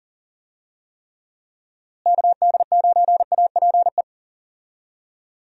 Morse code, heard as a keyed tone, about 40 wpm, 700 Hz. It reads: KD9APE
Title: USA Call Signs - 6 Characters - Repeated 2x - 40wpm